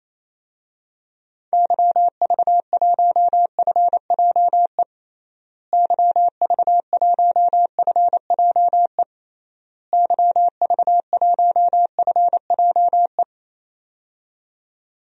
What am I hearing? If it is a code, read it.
Morse code, heard as a keyed tone, about 28 wpm, 700 Hz. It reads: YV1FJE YV1FJE YV1FJE
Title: DXCC-100 Call Signs - 6 Characters - Repeated 3x - 28wpm